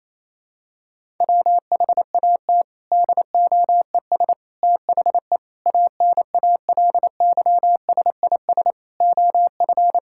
Morse code, heard as a keyed tone, about 28 wpm, 700 Hz. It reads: WHAT DOES THE ANALYSIS OF
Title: Sentences from Top 1000 Words - Review Part 6 - 28wpm